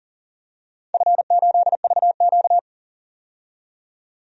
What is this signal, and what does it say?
Morse code, heard as a keyed tone, about 40 wpm, 700 Hz. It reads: F8VQ
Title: DXCC-100 Call Signs - 4 Characters - 40wpm